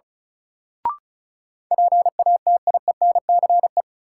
Morse code, heard as a keyed tone, about 34 wpm, 700 Hz. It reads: PATIENCE
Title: Sentences from Top 1300 Words - New Word Drill Alt - 35wpm